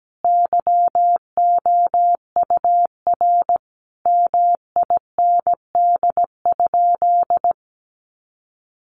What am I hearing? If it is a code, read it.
Morse code, heard as a keyed tone, about 17 wpm, 700 Hz. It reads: YOUR MIND?
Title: Sentences from Top 1300 Words 17wpm